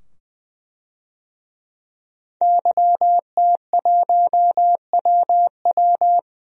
Morse code, heard as a keyed tone, about 21 wpm, 700 Hz. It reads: YT1WW
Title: DXCC-100 Call Signs - 5 Characters - 20wpm